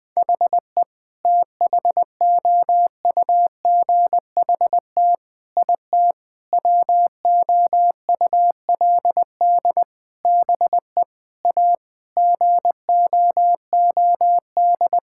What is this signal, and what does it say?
Morse code, heard as a keyed tone, about 20 wpm, 700 Hz. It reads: HE THOUGHT IT WOULD BE A GOOD
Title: Sentences with 8-Word Phrases - 20wpm